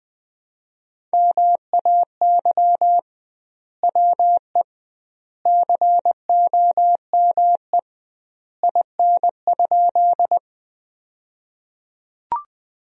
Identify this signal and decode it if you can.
Morse code, heard as a keyed tone, about 20 wpm, 700 Hz. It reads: MAY WE COME IN?
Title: 2x Word-Spacing - Sentences from Top 100 Words - 20wpm